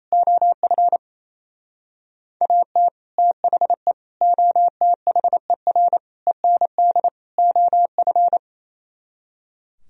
Morse code, heard as a keyed tone, about 34 wpm, 700 Hz. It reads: OF AT THE OTHER END OF
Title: Speed-Racing - Sets of 5 Words 28wpm